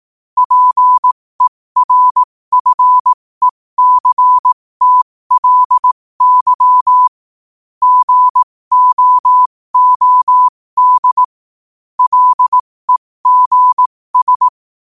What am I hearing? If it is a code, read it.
Morse code, heard as a keyed tone, about 18 wpm, spaced as for 15 wpm, 1000 Hz. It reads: PERFECTLY GOOD LEGS
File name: qotdfiles / 15wpm